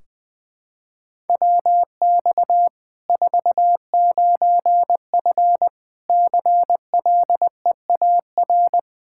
Morse code, heard as a keyed tone, about 20 wpm, 700 Hz. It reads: WX 49F CLEAR
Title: Practice QSO - Exchange 2 - Info - 20wpm